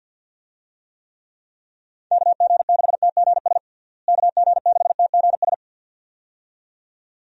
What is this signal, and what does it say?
Morse code, heard as a keyed tone, about 52 wpm, 700 Hz. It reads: KG6TGS KG6TGS
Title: USA Call Signs - 6 Characters - Repeated 2x - 50wpm